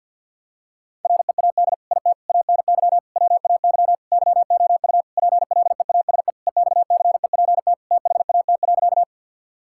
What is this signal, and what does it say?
Morse code, heard as a keyed tone, about 50 wpm, 700 Hz. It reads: READ IT ANY WAY YOU PLEASE EXCEPT THAT.
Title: Sentences from Top 1600 Words 50wpm